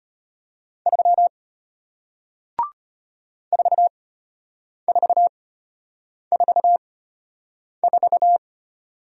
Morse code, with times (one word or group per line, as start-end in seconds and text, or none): 0.9-1.3 3
3.5-8.4 4444